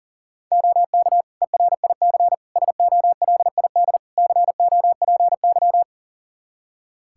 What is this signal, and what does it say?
Morse code, heard as a keyed tone, about 40 wpm, 700 Hz. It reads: OK ERIC SOLID COPY
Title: Practice QSO - Exchange 3 - More Info - 40wpm